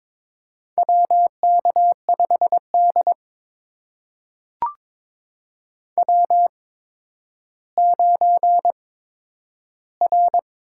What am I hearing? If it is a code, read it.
Morse code, heard as a keyed tone, about 23 wpm, spaced as for 11 wpm, 700 Hz. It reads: WK5D W9R
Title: USA Call Signs - 4 Characters - 8x Character Spacing - 22wpm